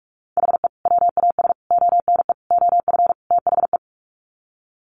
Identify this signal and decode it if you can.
Morse code, heard as a keyed tone, about 44 wpm, 700 Hz. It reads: HE WAS ONE OF THE
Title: Sets of 5 Words 45wpm